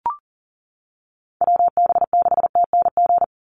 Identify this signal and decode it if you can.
Morse code, heard as a keyed tone, about 40 wpm, 700 Hz. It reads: WB6TNG
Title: USA Call Signs 40wpm